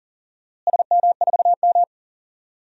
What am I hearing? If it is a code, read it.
Morse code, heard as a keyed tone, about 40 wpm, 700 Hz. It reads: SM4M
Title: DXCC-100 Call Signs - 4 Characters - Repeated 3x - 40wpm